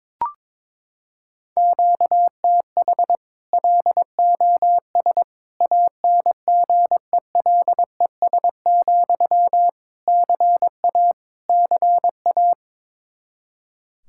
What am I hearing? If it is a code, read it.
Morse code, heard as a keyed tone, about 22 wpm, 700 Hz. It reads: QTH LOS ANGELES, CA CA
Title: Practice QSO - Exchange 1 - Beginning - 22wpm